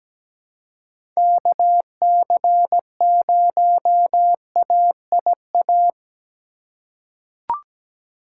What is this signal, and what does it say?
Morse code, heard as a keyed tone, about 17 wpm, 700 Hz. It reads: KC0AIA
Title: USA Call Signs - 6 Characters - 4x Character Spacing - 17wpm